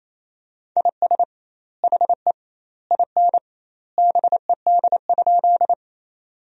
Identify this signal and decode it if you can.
Morse code, heard as a keyed tone, about 28 wpm, 700 Hz. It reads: IS HE IN BED?
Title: 2x Word-Spacing - Sentences from Top 500 Words - 28wpm